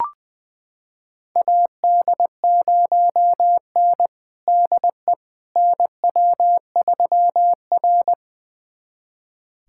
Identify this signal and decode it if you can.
Morse code, heard as a keyed tone, about 20 wpm, 700 Hz. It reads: AD0N DE NW3R
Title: USA Callsign DE Callsign - 4 Characters - 20wpm